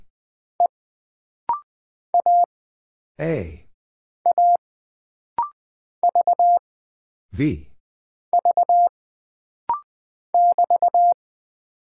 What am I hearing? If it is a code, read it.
Morse code, heard as a keyed tone, about 20 wpm, 700 Hz. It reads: E A A V V <BT>
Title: Single Character (Letter:Number:Punctuation:Prosign) - Rapid-Fire - 20wpm